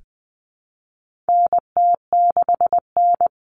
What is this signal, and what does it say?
Morse code, heard as a keyed tone, about 20 wpm, 700 Hz. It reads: NT6N
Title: All Cut Numbers - 4-Digit 20wpm